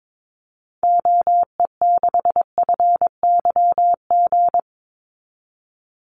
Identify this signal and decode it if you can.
Morse code, heard as a keyed tone, about 22 wpm, 700 Hz. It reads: OE6FYG
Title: Call Signs Repeated 3x - 22wpm